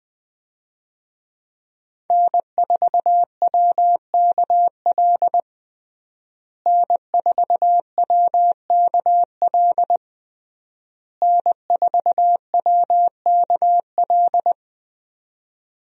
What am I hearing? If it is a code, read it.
Morse code, heard as a keyed tone, about 20 wpm, 700 Hz. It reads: N4WKL N4WKL N4WKL
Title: USA Call Signs - 5 Characters - Repeated 3x - 20wpm